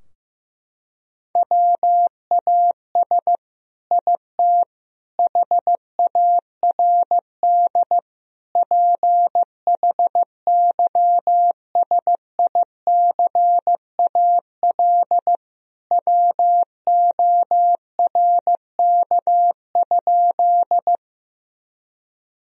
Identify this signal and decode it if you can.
Morse code, heard as a keyed tone, about 15 wpm, 700 Hz. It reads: WAS IT HARD PHYSICAL WORK?
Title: Sentences from Top 1000 Words - Review Part 4 - 15wpm